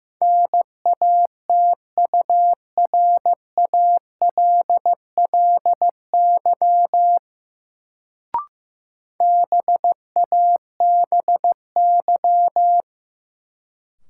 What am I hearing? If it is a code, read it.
Morse code, heard as a keyed tone, about 15 wpm, 700 Hz. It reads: NATURALLY BABY
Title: Sentences from Top 1300 Words - New Word Drill Alt - 15wpm